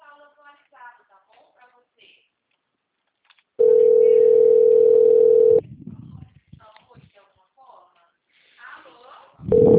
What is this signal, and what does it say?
Ringback tone (North America), 440 + 480 Hz, with speech.
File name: out- -2013- - - .253